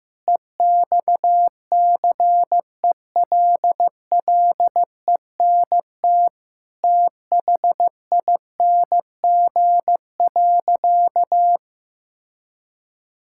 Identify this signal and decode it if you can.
Morse code, heard as a keyed tone, about 15 wpm, 700 Hz. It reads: EXCELLENT THING.
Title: Sentences from Top 1000 Words - Review Part 5 - 15wpm